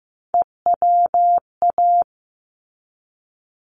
Morse code, heard as a keyed tone, about 15 wpm, 700 Hz. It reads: EWA
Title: ARRL Sections 15wpm